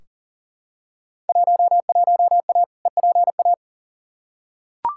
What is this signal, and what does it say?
Morse code, heard as a keyed tone, about 40 wpm, 700 Hz. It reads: 11A EPA
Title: Field Day Exchanges Encore - 40wpm